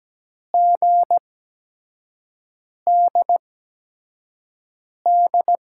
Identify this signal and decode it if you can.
Morse code, heard as a keyed tone, about 17 wpm, spaced as for 5 wpm, 700 Hz. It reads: GDD
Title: USA Call Signs - 5 Characters - 8x Character Spacing - 17wpm